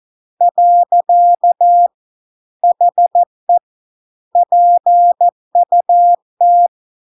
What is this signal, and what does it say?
Morse code, heard as a keyed tone, about 14 wpm, 700 Hz. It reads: . HE PUT